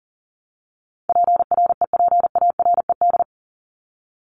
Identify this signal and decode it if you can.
Morse code, heard as a keyed tone, about 40 wpm, 700 Hz. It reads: PREPARED
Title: ED Suffix - Top 100 Words with Base Words 40wpm